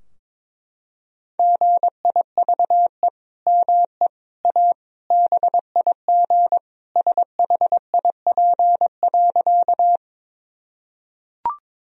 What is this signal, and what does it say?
Morse code, heard as a keyed tone, about 22 wpm, 700 Hz. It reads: GIVE ME A BIG SHIP.
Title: Sentences from Top 1000 Words - Review Part 2 - 22wpm